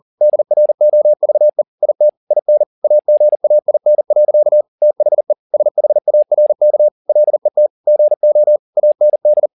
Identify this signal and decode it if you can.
Morse code, heard as a keyed tone, about 40 wpm, 600 Hz. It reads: DROVE IT IN AGAIN. THE SHARK LET GO AND
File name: the old man and the sea 2 40wpm